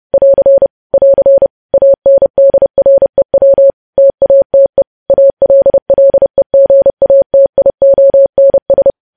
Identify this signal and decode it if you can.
Morse code, heard as a keyed tone, about 30 wpm, 550 Hz. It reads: <AR> <AR> ANDREW TATE ALLEGATIONS
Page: news headlines in morse code at 30 wpm